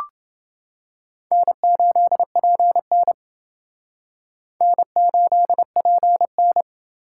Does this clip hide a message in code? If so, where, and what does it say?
Yes: Morse code from 1.3 to 6.6 s, heard as a keyed tone, about 28 wpm, 700 Hz. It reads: N8PN N8PN